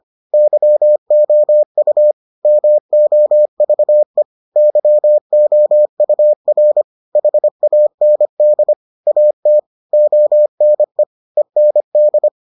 Morse code, heard as a keyed tone, about 25 wpm, 600 Hz. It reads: YOU MOVE YOUR HAND AT ONE END